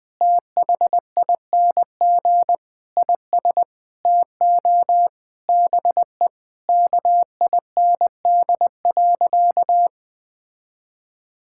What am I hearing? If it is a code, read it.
Morse code, heard as a keyed tone, about 21 wpm, 700 Hz. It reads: THING IS TO BE KIND.
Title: Sentences with 4-Word Phrases - Spoken Beginning - 20wpm